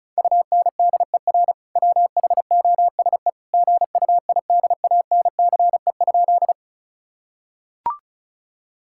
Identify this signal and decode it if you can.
Morse code, heard as a keyed tone, about 36 wpm, 700 Hz. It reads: UNDER WHOSE GUIDANCE?
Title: Sentences from Top 1600 Words 35wpm